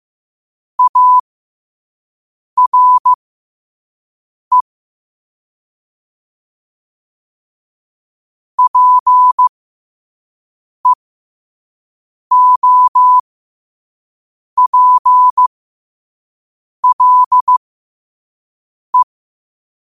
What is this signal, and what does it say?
Morse code, heard as a keyed tone, about 15 wpm, spaced as for 5 wpm, 1000 Hz. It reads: ARE PEOPLE
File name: qotdfiles / 5wpm